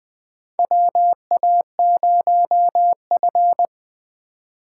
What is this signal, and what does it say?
Morse code, heard as a keyed tone, about 20 wpm, 700 Hz. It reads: WA0F
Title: USA Call Signs - 4 Characters - 4x Character Spacing - 20wpm